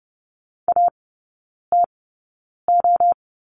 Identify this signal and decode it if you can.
Morse code, heard as a keyed tone, about 30 wpm, spaced as for 9 wpm, 690 Hz. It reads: ATO
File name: Lesson 17e - Letter u - Three Characters 30wpm